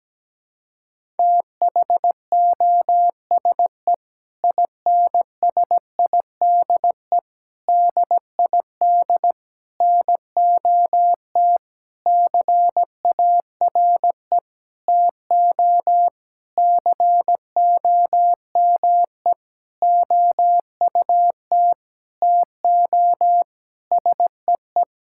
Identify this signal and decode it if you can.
Morse code, heard as a keyed tone, about 17 wpm, 700 Hz. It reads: THOSE INSIDE DID NOT CARE TO COME OUT TO SEE
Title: Sentences from Top 1000 Words - Review Part 6 - 17wpm